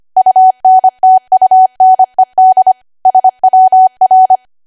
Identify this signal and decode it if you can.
Morse code, heard as a keyed tone, about 25 wpm, 750 Hz. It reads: UNTUNED SWR
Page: 25 WPM Code Archive